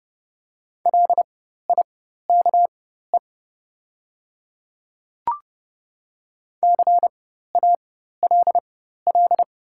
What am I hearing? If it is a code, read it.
Morse code, heard as a keyed tone, about 30 wpm, spaced as for 14 wpm, 700 Hz. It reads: LIKE CALL
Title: 4x Character Spacing - Top 200 Words - 30wpm